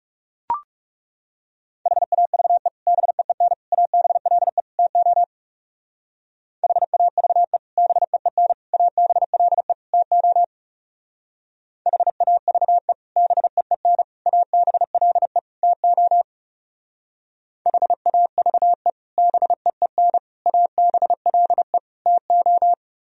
Morse code, heard as a keyed tone, about 39 wpm, 700 Hz. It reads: HAVE BEEN ABLE TO HAVE BEEN ABLE TO HAVE BEEN ABLE TO HAVE BEEN ABLE TO